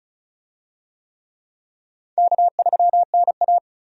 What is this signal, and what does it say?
Morse code, heard as a keyed tone, about 36 wpm, 700 Hz. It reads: K3NA
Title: USA Call Signs - 4 Characters - 35wpm